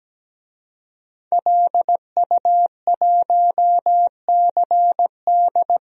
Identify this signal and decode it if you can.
Morse code, heard as a keyed tone, about 17 wpm, 700 Hz. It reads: LU1CD
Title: DXCC-100 Call Signs - 5 Characters - 17wpm